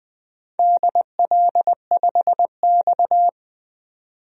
Morse code, heard as a keyed tone, about 20 wpm, 700 Hz. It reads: DL5X
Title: DXCC-100 Call Signs - 4 Characters - 20wpm